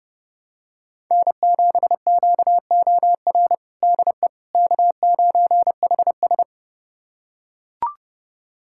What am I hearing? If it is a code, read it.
Morse code, heard as a keyed tone, about 32 wpm, 700 Hz. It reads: N7QOR DE K9HS
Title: USA Callsign DE Callsign 30wpm